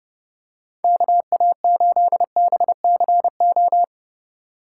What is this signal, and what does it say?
Morse code, heard as a keyed tone, about 30 wpm, 700 Hz. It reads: KA8BCO
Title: USA Call Signs - 2x Character Spacing - 30wpm